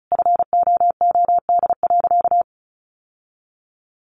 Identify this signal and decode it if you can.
Morse code, heard as a keyed tone, about 34 wpm, 700 Hz. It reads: FOOD.
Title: Sentences from Top 1200 Words 35wpm